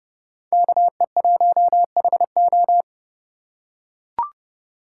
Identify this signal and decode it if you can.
Morse code, heard as a keyed tone, about 30 wpm, 690 Hz. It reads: KE1HO
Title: USA Call Signs - 5 Characters - 2x Character Spacing - 30wpm